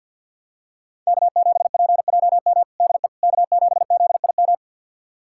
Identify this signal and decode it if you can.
Morse code, heard as a keyed tone, about 46 wpm, 700 Hz. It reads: K7PJM DE K7ZIM
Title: USA Callsign DE Callsign - 5 Characters - 50wpm